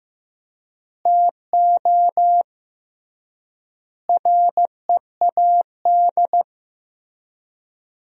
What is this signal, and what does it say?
Morse code, heard as a keyed tone, about 15 wpm, 700 Hz. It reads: TO READ
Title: 3x Word-Spacing - Sentences from Top 500 Words - 15wpm